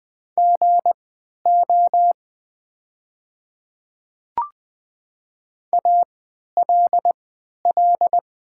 Morse code, heard as a keyed tone, about 20 wpm, spaced as for 11 wpm, 700 Hz. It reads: GO ALL
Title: 3x Character Spacing - Top 100 Words - 20wpm